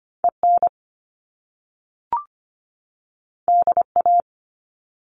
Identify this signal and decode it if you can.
Morse code, heard as a keyed tone, about 25 wpm, 700 Hz. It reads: EN DA
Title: Top Two Letter Combinations in Words 25wpm